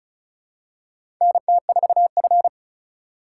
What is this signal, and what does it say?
Morse code, heard as a keyed tone, about 34 wpm, 700 Hz. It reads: NT4F
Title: USA Call Signs - 4 Characters - 2x Character Spacing - 35wpm